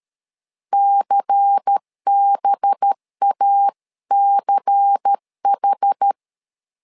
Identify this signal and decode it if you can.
Morse code, heard as a keyed tone, about 13 wpm, 800 Hz. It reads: CBACH
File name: LEZIONI / 5 LBVC / 5 LBVC 12wpm 1955